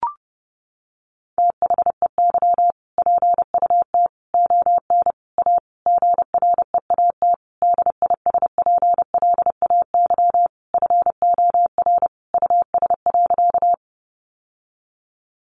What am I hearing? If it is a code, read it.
Morse code, heard as a keyed tone, about 30 wpm, 700 Hz. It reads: THEY PUT ON A GREAT DISPLAY FOR US.
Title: Sentences from Top 1200 Words 30wpm